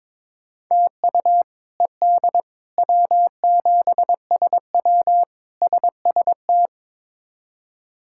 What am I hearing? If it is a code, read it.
Morse code, heard as a keyed tone, about 22 wpm, 700 Hz. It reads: TU ED W7SW SST
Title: K1USN SST QSO Practice - 22wpm